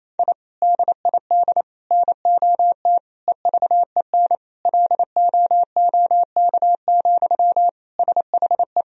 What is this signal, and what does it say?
Morse code, heard as a keyed tone, about 28 wpm, 700 Hz. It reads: I DID NOT EVEN LOOK, SHE
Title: Sentences from Top 1000 Words - Review Part 1 - 28wpm